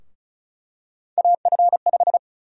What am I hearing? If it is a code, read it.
Morse code, heard as a keyed tone, about 36 wpm, 700 Hz. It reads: AF5
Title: USA Call Sign Prefixes - Repeated 2x - 35wpm